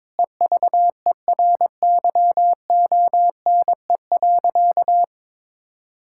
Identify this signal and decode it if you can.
Morse code, heard as a keyed tone, about 22 wpm, 700 Hz. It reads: EVERYONE.
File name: practice / 22wpm / Sentences with 8-Word Phrases - 22wpm